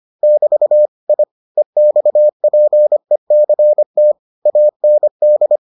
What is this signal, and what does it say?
Morse code, heard as a keyed tone, about 25 wpm, 600 Hz. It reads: <BT> I EXPECT AND